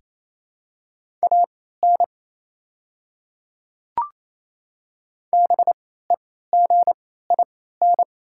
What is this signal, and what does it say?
Morse code, heard as a keyed tone, about 28 wpm, spaced as for 16 wpm, 700 Hz. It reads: AN BEGIN
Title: 3x Character Spacing - Top 500 Words - 28wpm